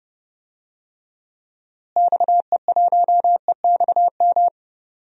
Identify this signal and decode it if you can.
Morse code, heard as a keyed tone, about 30 wpm, 700 Hz. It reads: XE1EXM
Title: DXCC-100 Call Signs - 6 Characters - Repeated 3x - 30wpm